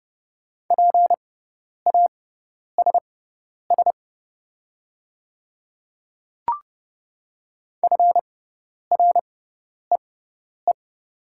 Morse code, heard as a keyed tone, about 29 wpm, spaced as for 10 wpm, 700 Hz. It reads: PASS FREE
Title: 6x Character Spacing - Top 500 Words - 30wpm